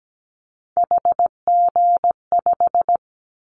Morse code, heard as a keyed tone, about 17 wpm, 700 Hz. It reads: HG5
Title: Callsign Prefixes for Top 25 DXCC - 17wpm